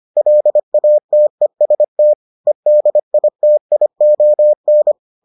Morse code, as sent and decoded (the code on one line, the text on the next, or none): .-.. .- - . ... - / . -.. .. - .. --- -.
LATEST EDITION